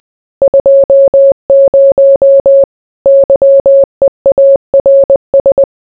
Morse code, heard as a keyed tone, about 20 wpm, 550 Hz. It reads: 20 YEARS